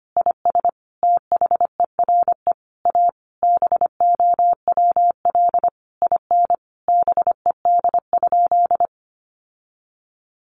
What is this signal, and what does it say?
Morse code, heard as a keyed tone, about 25 wpm, 700 Hz. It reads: IS THERE A BOWL IN BED?
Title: Sentences from Top 1200 Words 25wpm